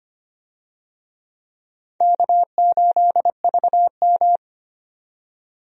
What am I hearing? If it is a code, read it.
Morse code, heard as a keyed tone, about 25 wpm, 700 Hz. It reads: K8VM
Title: USA Call Signs - 4 Characters - 25wpm